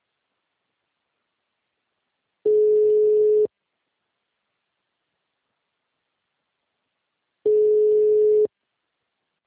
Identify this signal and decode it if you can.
Ringback tone (Europe, 425 Hz).